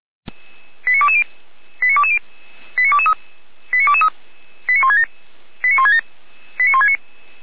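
Five-tone selective call (ZVEI), heard as quick runs of five tones, tones of about 70 ms. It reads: 89220 89220 89223 89167 89167 89169
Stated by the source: ZVEI five-tone selective call 89121 received with an SDR.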